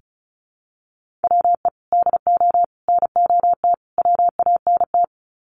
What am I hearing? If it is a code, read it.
Morse code, heard as a keyed tone, about 34 wpm, 700 Hz. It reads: WE DO NOT WANT
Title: Sets of 4 Words 35wpm